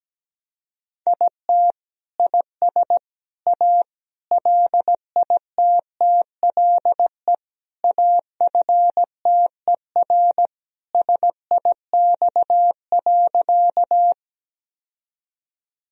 Morse code, heard as a keyed tone, about 17 wpm, 700 Hz. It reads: IT IS A LITTLE AFTER SIX.
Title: Sentences from Top 1000 Words - Review Part 2 - 17wpm